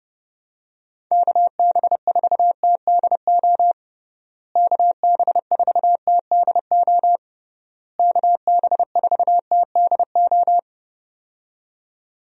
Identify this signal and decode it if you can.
Morse code, heard as a keyed tone, about 31 wpm, 700 Hz. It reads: KB4TDO KB4TDO KB4TDO
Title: USA Call Signs - 6 Characters - Repeated 3x - 30wpm